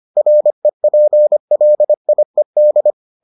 Morse code, heard as a keyed tone, about 25 wpm, 600 Hz. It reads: REPLIED